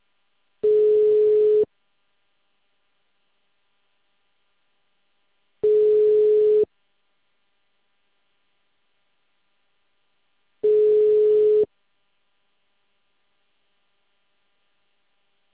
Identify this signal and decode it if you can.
Ringback tone (Europe, 425 Hz).